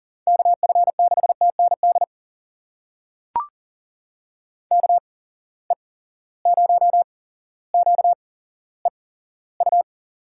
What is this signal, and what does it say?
Morse code, heard as a keyed tone, about 40 wpm, spaced as for 17 wpm, 700 Hz. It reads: KF6TND KE0QEU
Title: USA Call Signs - 6 Characters - 8x Character Spacing - 40wpm